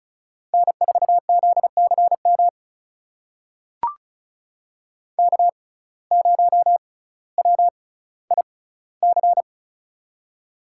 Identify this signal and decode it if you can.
Morse code, heard as a keyed tone, about 36 wpm, spaced as for 18 wpm, 700 Hz. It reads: N4ZCM K0WIC
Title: USA Call Signs - 5 Characters - 6x Character Spacing - 35wpm